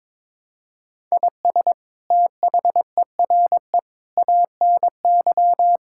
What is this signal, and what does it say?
Morse code, heard as a keyed tone, about 22 wpm, 700 Hz. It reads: IS THERE ANY
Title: Sentences from Top 1000 Words - Review Part 6 - 22wpm